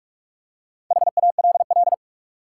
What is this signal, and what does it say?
Morse code, heard as a keyed tone, about 46 wpm, 700 Hz. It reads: HALL
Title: Four Letter Words - 45wpm